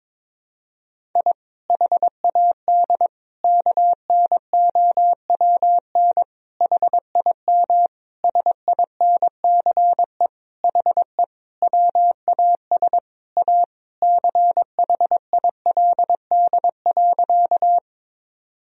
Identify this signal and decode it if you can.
Morse code, heard as a keyed tone, about 22 wpm, 700 Hz. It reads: I HAD KNOWN HIM SINCE HE WAS A CHILD.
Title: Sentences from Top 1000 Words - Review Part 3 - 22wpm